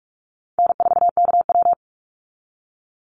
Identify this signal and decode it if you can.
Morse code, heard as a keyed tone, about 46 wpm, 700 Hz. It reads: N4KW
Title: Easy Call Signs - 45wpm